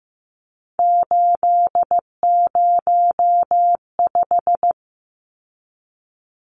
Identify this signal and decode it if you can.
Morse code, heard as a keyed tone, about 15 wpm, 700 Hz. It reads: 805
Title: Numbers - 3-Digit 15wpm